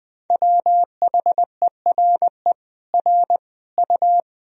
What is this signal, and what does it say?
Morse code, heard as a keyed tone, about 20 wpm, 700 Hz. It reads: WHERE R U
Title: Sentences from Top 200 Words and Abbreviations 20wpm